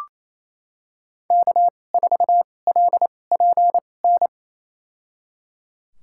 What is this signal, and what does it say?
Morse code, heard as a keyed tone, about 28 wpm, spaced as for 20 wpm, 700 Hz. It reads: K4LPN
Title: USA Call Signs - 2x Character Spacing - 28wpm